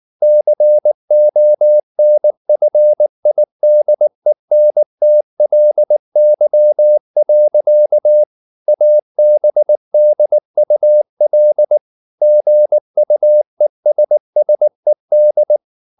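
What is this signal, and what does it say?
Morse code, heard as a keyed tone, about 19 wpm, 600 Hz. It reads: CONFIDENTLY. ABDUL GUESSED
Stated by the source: Morse code at 19 wpm from CW med Gnister, reading Ken Follett's novel Never in English.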